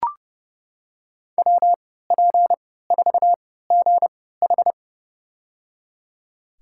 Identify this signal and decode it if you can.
Morse code, heard as a keyed tone, about 30 wpm, spaced as for 17 wpm, 700 Hz. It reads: WP4GH
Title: USA Call Signs - 5 Characters - 3x Character Spacing - 30wpm